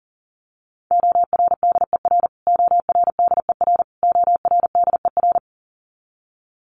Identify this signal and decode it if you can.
Morse code, heard as a keyed tone, about 40 wpm, 700 Hz. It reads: ORDER ORDER ORDER
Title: Top 500 Words Repeated 40wpm